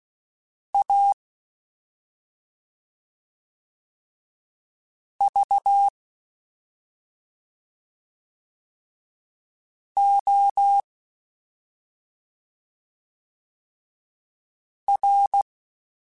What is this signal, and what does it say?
Morse code, heard as a keyed tone, about 16 wpm, spaced as for 2 wpm, 800 Hz. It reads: AVOR
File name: Special INGs-mp3s / Special INGs-mp3s / Slow ing @10